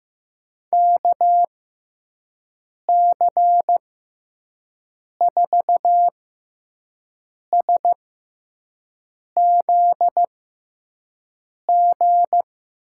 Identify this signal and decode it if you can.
Morse code, heard as a keyed tone, about 15 wpm, spaced as for 5 wpm, 700 Hz. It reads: KC4SZG
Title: USA Call Signs - 6 Characters - 6x Character Spacing - 15wpm